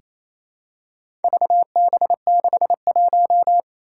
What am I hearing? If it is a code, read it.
Morse code, heard as a keyed tone, about 27 wpm, 700 Hz. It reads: VB61
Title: Four Challenging Letter-Number Combinations 28wpm